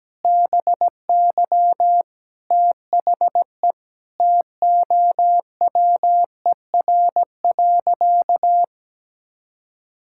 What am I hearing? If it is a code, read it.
Morse code, heard as a keyed tone, about 17 wpm, 700 Hz. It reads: BY THE TOWER.
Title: Pangram Challenge v2 - 17wpm